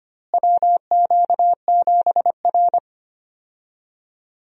Morse code, heard as a keyed tone, about 25 wpm, 700 Hz. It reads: WQ7R
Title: USA Call Signs - 4 Characters - Repeated 3x - 25wpm